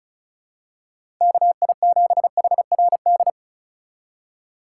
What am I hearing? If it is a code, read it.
Morse code, heard as a keyed tone, about 36 wpm, 700 Hz. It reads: KI7HRD
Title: USA Call Signs - 6 Characters - 3x Character Spacing - 35wpm